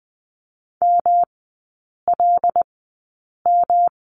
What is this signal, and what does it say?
Morse code, heard as a keyed tone, about 20 wpm, spaced as for 8 wpm, 700 Hz. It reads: MLM
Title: 152 Lesson 25e - Letter p - Three Characters 20wpm